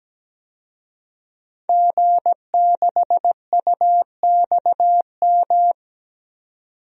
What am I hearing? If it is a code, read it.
Morse code, heard as a keyed tone, about 17 wpm, 700 Hz. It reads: G6UXM
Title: DXCC-100 Call Signs - 5 Characters - 17wpm